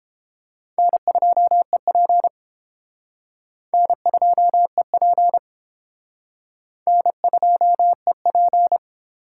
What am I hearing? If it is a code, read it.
Morse code, heard as a keyed tone, about 30 wpm, 700 Hz. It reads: N2EP N2EP N2EP